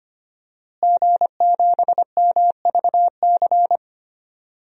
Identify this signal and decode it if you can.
Morse code, heard as a keyed tone, about 25 wpm, 700 Hz. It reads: G7MVC
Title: DXCC-100 Call Signs - 5 Characters - Repeated 3x - 25wpm